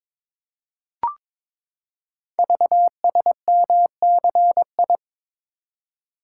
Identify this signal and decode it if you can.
Morse code, heard as a keyed tone, about 22 wpm, 700 Hz. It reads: VSMCI
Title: Code Groups - 5 Letter:Numbers - 22wpm